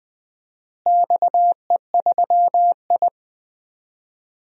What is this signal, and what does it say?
Morse code, heard as a keyed tone, about 20 wpm, 700 Hz. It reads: XE3I
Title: DXCC-100 Call Signs - 4 Characters - Repeated 3x - 20wpm